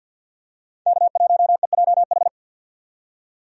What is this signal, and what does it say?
Morse code, heard as a keyed tone, about 51 wpm, 700 Hz. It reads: K1EJH
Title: USA Call Signs - 5 Characters - 50wpm